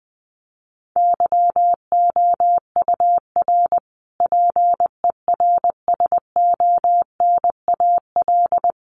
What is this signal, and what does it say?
Morse code, heard as a keyed tone, about 20 wpm, 700 Hz. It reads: YOUR PERSONAL
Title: A Sound Of Thunder 20wpm